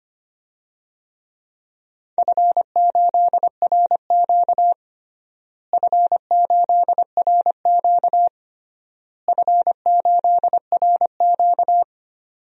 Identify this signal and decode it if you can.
Morse code, heard as a keyed tone, about 25 wpm, 700 Hz. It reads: F8RQ F8RQ F8RQ
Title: DXCC-100 Call Signs - 4 Characters - Repeated 3x - 25wpm